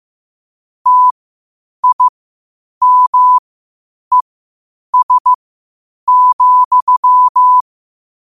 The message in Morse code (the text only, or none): TIMES,